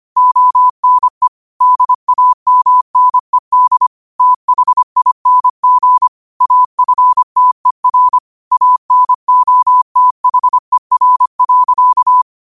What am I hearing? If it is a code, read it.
Morse code, heard as a keyed tone, about 25 wpm, 1000 Hz. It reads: ONE DAMNED THING AFTER ANOTHER.
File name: qotdfiles / 25wpm